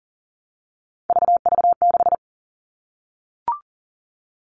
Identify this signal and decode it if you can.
Morse code, heard as a keyed tone, about 40 wpm, 700 Hz. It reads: VV6
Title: All Cut Numbers - 3-Digit 40wpm